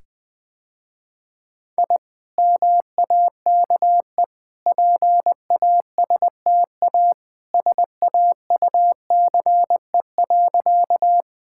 Morse code, heard as a keyed tone, about 21 wpm, 700 Hz. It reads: I MAKE PASTA SAUCE.
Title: Easy Sentences with Hints - 20wpm